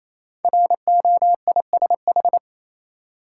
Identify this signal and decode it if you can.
Morse code, heard as a keyed tone, about 27 wpm, 700 Hz. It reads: ROISH
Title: Code Groups - 5 Letters - 28wpm